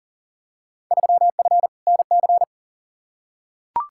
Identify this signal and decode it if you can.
Morse code, heard as a keyed tone, about 39 wpm, 700 Hz. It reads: 3F NC
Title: Field Day Exchanges Encore - 40wpm